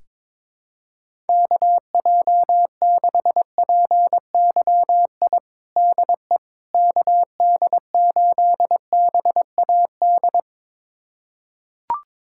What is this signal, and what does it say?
Morse code, heard as a keyed tone, about 21 wpm, 700 Hz. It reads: KJ6PYI DE KD8BAD
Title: USA Callsign DE Callsign - 6 Characters - 22wpm